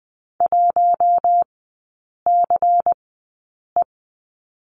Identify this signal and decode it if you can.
Morse code, heard as a keyed tone, about 20 wpm, spaced as for 8 wpm, 700 Hz. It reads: 1CE
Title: 127 Lesson 21e - Number 6 - Three Characters 20wpm